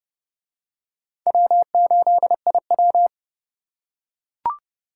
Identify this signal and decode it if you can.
Morse code, heard as a keyed tone, about 27 wpm, 700 Hz. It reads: W8IW